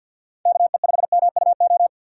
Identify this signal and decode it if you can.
Morse code, heard as a keyed tone, about 52 wpm, 700 Hz. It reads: KE5MUO
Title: USA Call Signs - 6 Characters - 6x Character Spacing - 50wpm